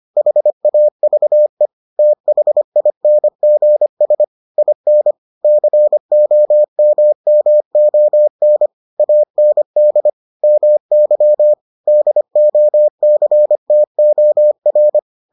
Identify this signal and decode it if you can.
Morse code, heard as a keyed tone, about 25 wpm, 600 Hz. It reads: HAVE THINGS IN COMMON AND MY DOCTOR